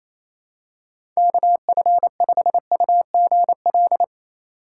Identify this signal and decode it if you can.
Morse code, heard as a keyed tone, about 28 wpm, 700 Hz. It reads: KF5UGL
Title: USA Call Signs - 4x Character Spacing - 28wpm